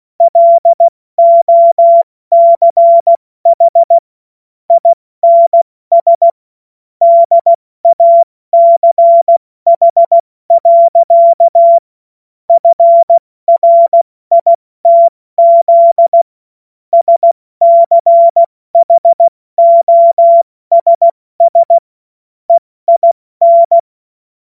Morse code, heard as a keyed tone, about 16 wpm, 680 Hz. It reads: LOCH INS DACH. FRITZ SCHOSS EIN
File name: Morsekurs / Witze / Witz 07. September 2021, 70 F80